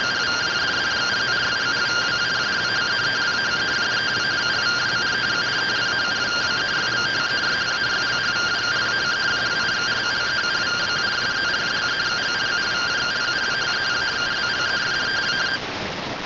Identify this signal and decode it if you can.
Two-tone teleprinter signal, about 170 Hz shift.